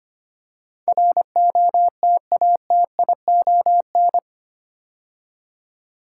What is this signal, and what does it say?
Morse code, heard as a keyed tone, about 25 wpm, 700 Hz. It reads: ROTATION
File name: 8 Letter Words - First 3 Letters Given - 25wpm